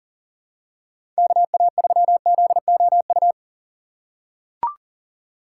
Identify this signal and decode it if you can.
Morse code, heard as a keyed tone, about 40 wpm, 700 Hz. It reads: KA3ZOU
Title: USA Call Signs - 6 Characters - 6x Character Spacing - 40wpm